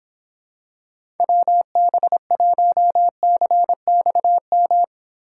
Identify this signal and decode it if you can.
Morse code, heard as a keyed tone, about 26 wpm, 700 Hz. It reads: WB1CXM